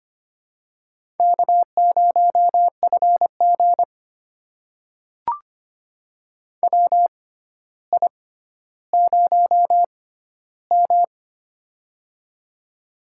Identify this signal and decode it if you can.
Morse code, heard as a keyed tone, about 25 wpm, spaced as for 13 wpm, 700 Hz. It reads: K0FG WI0M
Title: USA Call Signs - 4 Characters - 6x Character Spacing - 25wpm